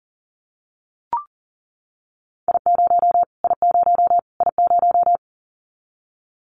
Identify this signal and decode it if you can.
Morse code, heard as a keyed tone, about 40 wpm, 700 Hz. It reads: I0 I0 I0
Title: Callsign Prefixes for Top 25 DXCC - Repeated 3x - 40wpm